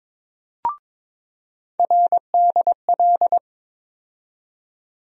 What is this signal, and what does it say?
Morse code, heard as a keyed tone, about 22 wpm, 700 Hz. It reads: RDL
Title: USA Call Sign Postfixes - 3 Letters 22wpm